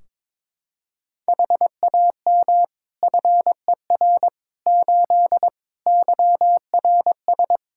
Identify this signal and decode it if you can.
Morse code, heard as a keyed tone, about 22 wpm, 700 Hz. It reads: HAM FER 8 YRS
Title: Practice QSO - Exchange 3 - More Info - 22wpm